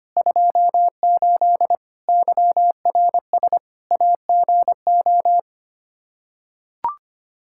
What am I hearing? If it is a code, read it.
Morse code, heard as a keyed tone, about 25 wpm, 700 Hz. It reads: 28 YRS AGO
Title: Practice QSO - Exchange 3 - More Info - 25wpm